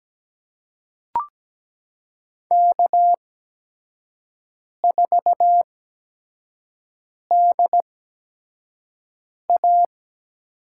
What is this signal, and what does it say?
Morse code, heard as a keyed tone, about 17 wpm, spaced as for 5 wpm, 700 Hz. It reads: K4DA
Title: USA Call Signs - 8x Character Spacing - 17wpm